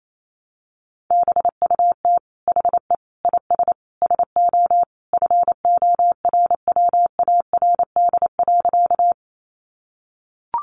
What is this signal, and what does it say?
Morse code, heard as a keyed tone, about 28 wpm, 700 Hz. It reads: BUT HE IS SO FORWARD.
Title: Sentences from Top 1300 Words 28wpm